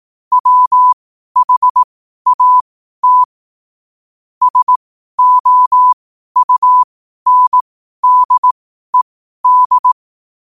Morse code, heard as a keyed tone, about 18 wpm, spaced as for 12 wpm, 1000 Hz. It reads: WHAT SOUNDED
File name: qotdfiles / 12wpm